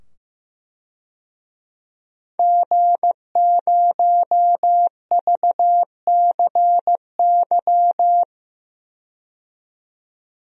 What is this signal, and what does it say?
Morse code, heard as a keyed tone, about 15 wpm, 700 Hz. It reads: G0VCY
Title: DXCC-100 Call Signs - 5 Characters - 15wpm